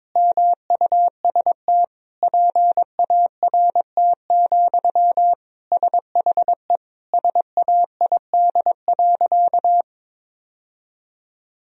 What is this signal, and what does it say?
Morse code, heard as a keyed tone, about 22 wpm, 700 Hz. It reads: MUST PART, SHE SAID.
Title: Sentences from Top 1000 Words - Review Part 1 - 22wpm